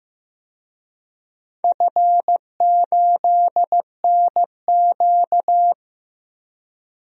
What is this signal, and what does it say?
Morse code, heard as a keyed tone, about 16 wpm, 700 Hz. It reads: F8NQ
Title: DXCC-100 Call Signs - 4 Characters - Repeated 3x - 15wpm